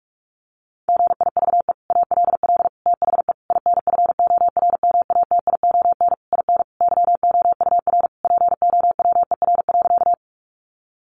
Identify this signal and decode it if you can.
Morse code, heard as a keyed tone, about 44 wpm, 700 Hz. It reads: GIVE ALL THE INFORMATION IN YOUR POWER.
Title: Sentences from Top 600 Words 45wpm